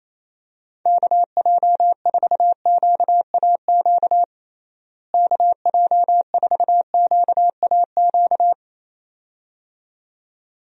Morse code, heard as a keyed tone, about 27 wpm, 700 Hz. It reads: KJ4QAQ KJ4QAQ
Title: USA Call Signs - 6 Characters - Repeated 2x - 28wpm